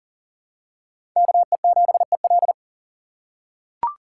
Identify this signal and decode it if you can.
Morse code, heard as a keyed tone, about 40 wpm, 700 Hz. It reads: KE7EL
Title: USA Call Signs - 5 Characters - 4x Character Spacing - 40wpm